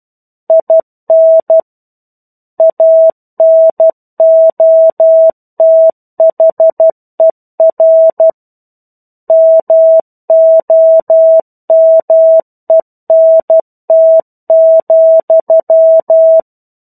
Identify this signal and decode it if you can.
Morse code, heard as a keyed tone, about 12 wpm, 650 Hz. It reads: IN ANOTHER MOMENT,
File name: 01-Chapter-01 12 WPM